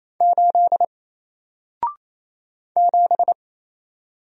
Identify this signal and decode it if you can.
Morse code, heard as a keyed tone, about 27 wpm, 700 Hz. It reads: 8 7
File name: Single-Letter-Number-28wpm